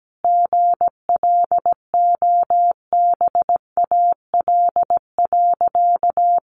Morse code, heard as a keyed tone, about 17 wpm, 700 Hz. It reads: GLOBAL.
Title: Sentences from Top 1200 Words 17wpm